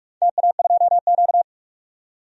Morse code, heard as a keyed tone, about 42 wpm, 700 Hz. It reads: TA2Q
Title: DXCC-100 Call Signs - 4 Characters - 45wpm